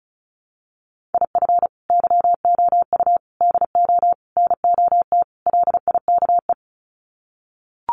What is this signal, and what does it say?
Morse code, heard as a keyed tone, about 34 wpm, 700 Hz. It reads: IF YOU DO NOT LIKE
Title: Sets of 5 Words 35wpm